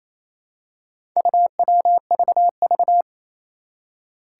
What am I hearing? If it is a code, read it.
Morse code, heard as a keyed tone, about 28 wpm, 700 Hz. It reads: UWVV
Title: Four Challenging Letter-Number Combinations 28wpm